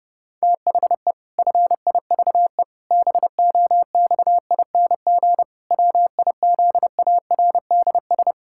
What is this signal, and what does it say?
Morse code, heard as a keyed tone, about 30 wpm, 700 Hz. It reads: THE FIVE BOXING WIZARDS
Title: Pangram Challenge v2 - 30wpm